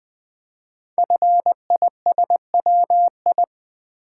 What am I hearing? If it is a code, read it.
Morse code, heard as a keyed tone, about 21 wpm, 700 Hz. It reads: FISWI
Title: Code Groups - 5 Letters - 20wpm